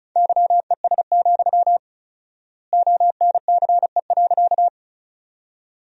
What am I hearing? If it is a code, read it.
Morse code, heard as a keyed tone, about 34 wpm, 700 Hz. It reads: YES, ONCE.
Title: 4x Word-Spacing - Sentences from Top 500 Words - 35wpm